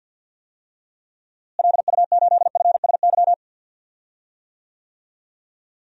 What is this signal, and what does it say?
Morse code, heard as a keyed tone, about 47 wpm, 700 Hz. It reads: LU8FSY
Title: DXCC-100 Call Signs - 6 Characters - Repeated 2x - 50wpm